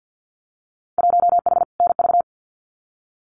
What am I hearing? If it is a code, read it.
Morse code, heard as a keyed tone, about 50 wpm, 700 Hz. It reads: 1H NV
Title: Winter Field Day Exchanges 50wpm